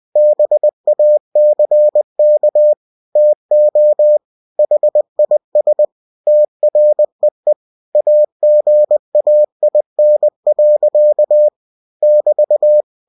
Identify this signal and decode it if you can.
Morse code, heard as a keyed tone, about 20 wpm, 600 Hz. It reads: BACK TO HIS TREE AGAIN. <BT>
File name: cw-books / winnie-the-pooh / winnie-20wpm-page-0094